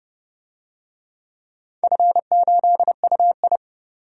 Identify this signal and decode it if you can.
Morse code, heard as a keyed tone, about 31 wpm, 700 Hz. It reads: F8UI
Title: DXCC-100 Call Signs - 4 Characters - 30wpm